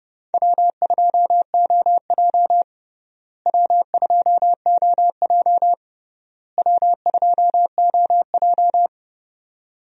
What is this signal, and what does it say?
Morse code, heard as a keyed tone, about 31 wpm, 700 Hz. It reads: W2OJ W2OJ W2OJ
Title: USA Call Signs - 4 Characters - Repeated 3x - 30wpm